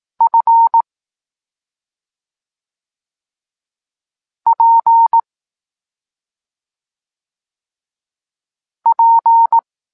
Morse code, heard as a keyed tone, about 18 wpm, 920 Hz. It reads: F P P
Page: Tekst 25 – Nauka liter P F